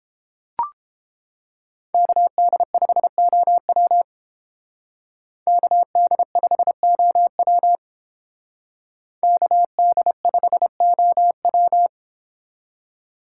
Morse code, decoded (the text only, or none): KD5OW KD5OW KD5OW